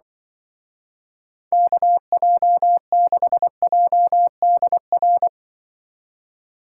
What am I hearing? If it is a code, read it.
Morse code, heard as a keyed tone, about 23 wpm, 700 Hz. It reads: KJ6JDR
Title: Speed Racing - USA 2x3 Call Signs 20wpm